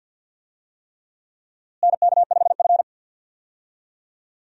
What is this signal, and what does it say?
Morse code, heard as a keyed tone, about 52 wpm, 700 Hz. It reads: NK5F
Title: USA Call Signs - 4 Characters - Repeated 3x - 50wpm